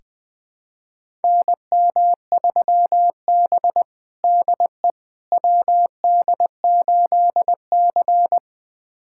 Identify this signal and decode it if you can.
Morse code, heard as a keyed tone, about 21 wpm, 700 Hz. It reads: NM3B DE WD8C
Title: USA Callsign DE Callsign - 4 Characters - 20wpm